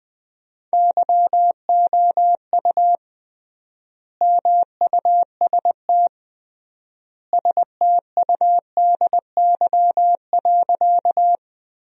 Morse code, heard as a keyed tone, about 20 wpm, 700 Hz. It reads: YOU MUST STUDY.
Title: 3x Word-Spacing - Sentences from Top 300 Words - 20wpm